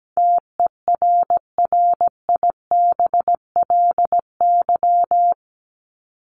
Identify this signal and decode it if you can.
Morse code, heard as a keyed tone, about 17 wpm, 700 Hz. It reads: TERRIBLY
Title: Top 1751-2000 Words 17wpm